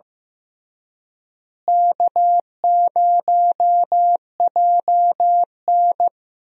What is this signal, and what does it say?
Morse code, heard as a keyed tone, about 15 wpm, 700 Hz. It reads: K0JN
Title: USA Call Signs - 4 Characters - Repeated 3x - 15wpm